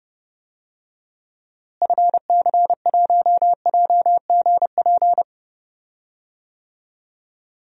Morse code, heard as a keyed tone, about 31 wpm, 700 Hz. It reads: FC1JGP
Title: DXCC-100 Call Signs - 6 Characters - Repeated 3x - 30wpm